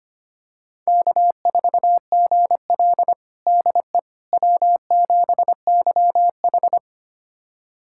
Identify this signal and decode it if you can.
Morse code, heard as a keyed tone, about 25 wpm, 700 Hz. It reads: K4GL DE W7YH
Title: USA Callsign DE Callsign - 4 Characters - 25wpm